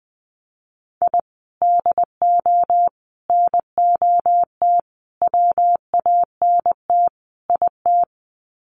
Morse code, heard as a keyed tone, about 20 wpm, 700 Hz. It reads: I DO NOT WANT IT
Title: Sets of 5 Words Encore2 20wpm